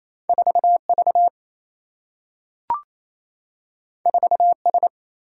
Morse code, heard as a keyed tone, about 28 wpm, 700 Hz. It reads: 4 V 4 S
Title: Single-Letter Single-Number 28wpm